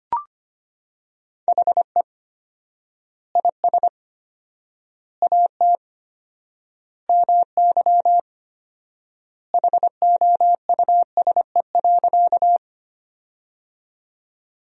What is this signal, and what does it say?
Morse code, heard as a keyed tone, about 25 wpm, 700 Hz. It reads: HE IS AT MY HOUSE.
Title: 4x Word-Spacing - Sentences from Top 200 Words - 25wpm